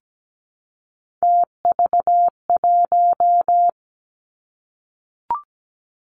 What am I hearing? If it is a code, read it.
Morse code, heard as a keyed tone, about 17 wpm, 700 Hz. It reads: TV1
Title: Three Challenging Letter-Number Combinations 17wpm